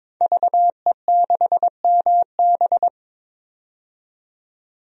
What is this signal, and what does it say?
Morse code, heard as a keyed tone, about 22 wpm, 700 Hz. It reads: VE6MB
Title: DXCC-100 Call Signs - 5 Characters - 22wpm